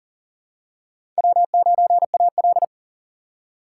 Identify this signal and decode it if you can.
Morse code, heard as a keyed tone, about 38 wpm, 700 Hz. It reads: W9AL
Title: USA Call Signs - 4 Characters - 2x Character Spacing - 40wpm